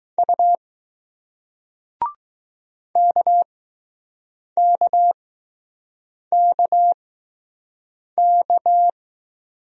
Morse code, heard as a keyed tone, about 19 wpm, spaced as for 6 wpm, 700 Hz. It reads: U KKKK